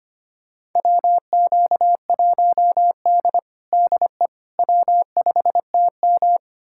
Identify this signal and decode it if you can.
Morse code, heard as a keyed tone, about 25 wpm, 700 Hz. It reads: WQ1D DE W5TM
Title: USA Callsign DE Callsign - 4 Characters - 25wpm